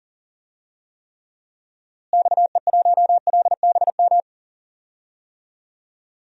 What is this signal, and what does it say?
Morse code, heard as a keyed tone, about 40 wpm, 700 Hz. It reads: XE1LBM
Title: DXCC-100 Call Signs - 6 Characters - Repeated 2x - 40wpm